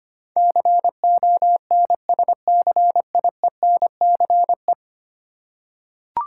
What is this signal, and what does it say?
Morse code, heard as a keyed tone, about 25 wpm, 700 Hz. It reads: CONSCIENCE
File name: 10 Letter Words - First 3 Letters Given - 25wpm